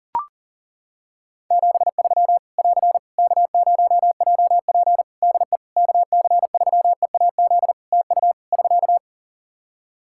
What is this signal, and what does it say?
Morse code, heard as a keyed tone, about 40 wpm, 700 Hz. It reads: 73 <AR> K0JP DE KC3EAZ TU <SK>
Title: Practice QSO - Exchange 4 - Ending - 40wpm